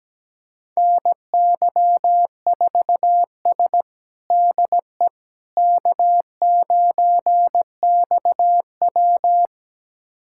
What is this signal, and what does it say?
Morse code, heard as a keyed tone, about 17 wpm, 700 Hz. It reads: NY4S DE K9XW
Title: USA Callsign DE Callsign - 4 Characters - 17wpm